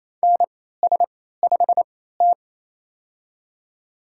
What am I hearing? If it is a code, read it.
Morse code, heard as a keyed tone, about 28 wpm, spaced as for 16 wpm, 700 Hz. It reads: NS5T
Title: USA Call Signs - 3x Character Spacing - 28wpm